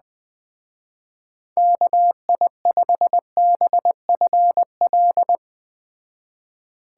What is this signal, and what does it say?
Morse code, heard as a keyed tone, about 20 wpm, 700 Hz. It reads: KI5BFL
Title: Speed Racing - USA Call Signs 17wpm